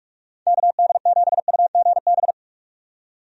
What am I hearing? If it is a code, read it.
Morse code, heard as a keyed tone, about 43 wpm, 700 Hz. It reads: KD7UGB
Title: USA Call Signs - 6 Characters - 45wpm